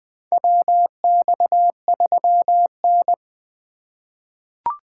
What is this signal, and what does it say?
Morse code, heard as a keyed tone, about 20 wpm, 700 Hz. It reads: WX3N
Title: USA Call Signs - 4 Characters - 4x Character Spacing - 20wpm